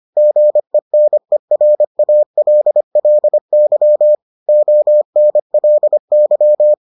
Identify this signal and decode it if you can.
Morse code, heard as a keyed tone, about 25 wpm, 600 Hz. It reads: GENERALLY ONLY